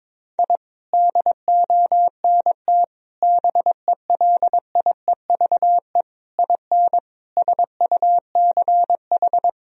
Morse code, heard as a keyed tone, about 22 wpm, 700 Hz. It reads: I DONT BELIEVE IN SUCH
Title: Sentences from Top 1000 Words - Review Part 2 - 22wpm